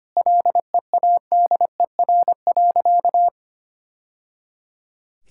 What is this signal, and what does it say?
Morse code, heard as a keyed tone, about 25 wpm, 700 Hz. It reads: LEADER.
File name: practice / 25wpm / Sentences with 3-Word Phrases - 25wpm